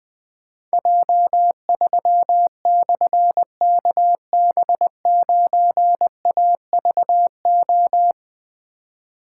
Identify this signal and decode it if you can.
Morse code, heard as a keyed tone, about 20 wpm, 700 Hz. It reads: J3/KB9AVO
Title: Difficult Call Signs 20wpm